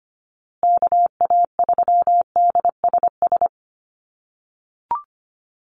Morse code, heard as a keyed tone, about 25 wpm, 700 Hz. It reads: KA3DSS
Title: USA Call Signs Repeated 3x 25wpm